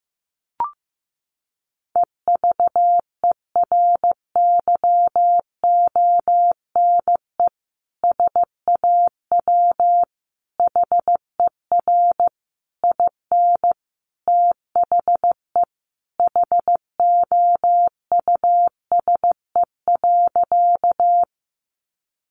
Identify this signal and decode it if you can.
Morse code, heard as a keyed tone, about 14 wpm, 700 Hz. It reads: EVERYONE SAW HER IN THE HOUSE.
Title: Sentences from Top 800 Words 15wpm